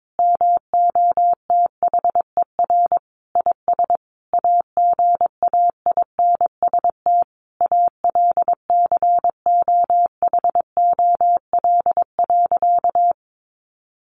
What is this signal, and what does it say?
Morse code, heard as a keyed tone, about 22 wpm, 700 Hz. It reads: MOTHER IS AGAINST ALCOHOL.
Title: Sentences from Top 1400 Words 22wpm